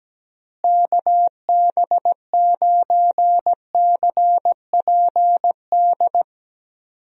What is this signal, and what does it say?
Morse code, heard as a keyed tone, about 17 wpm, 700 Hz. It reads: KB9CPD
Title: USA Call Signs - 6 Characters - Repeated 3x - 17wpm